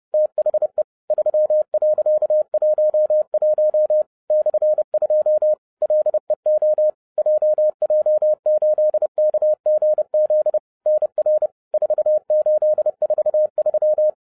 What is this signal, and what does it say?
Morse code, heard as a keyed tone, about 30 wpm, 610 Hz. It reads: THE 3.11 /2 LEO JJ8KGZ NR 4843